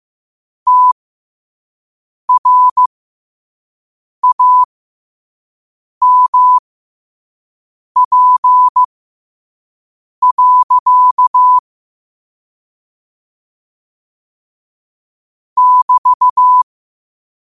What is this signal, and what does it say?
Morse code, heard as a keyed tone, about 15 wpm, spaced as for 5 wpm, 1000 Hz. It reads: TRAMP. <BT>
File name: qotdfiles / 5wpm